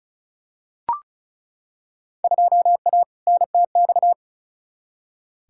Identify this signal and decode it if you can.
Morse code, heard as a keyed tone, about 34 wpm, 700 Hz. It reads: 2A NTX
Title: Field Day Exchanges - 35wpm